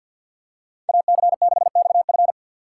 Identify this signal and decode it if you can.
Morse code, heard as a keyed tone, about 47 wpm, 700 Hz. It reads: AC6XF
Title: USA Call Signs - 5 Characters - 2x Character Spacing - 50wpm